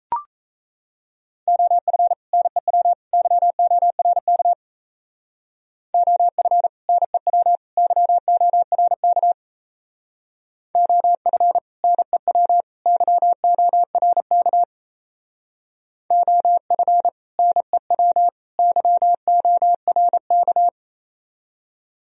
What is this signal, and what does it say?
Morse code, heard as a keyed tone, about 36 wpm, 700 Hz. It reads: OF NEW YORK OF NEW YORK OF NEW YORK OF NEW YORK